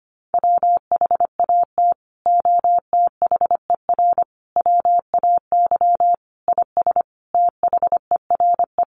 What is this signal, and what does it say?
Morse code, heard as a keyed tone, about 25 wpm, 700 Hz. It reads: WHAT OTHER WAY IS THERE
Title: Sentences from Top 100 Words 25wpm